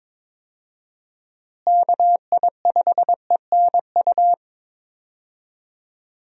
Morse code, heard as a keyed tone, about 22 wpm, 700 Hz. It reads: KI5ENU
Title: USA Call Signs - 6 Characters - Repeated 2x - 22wpm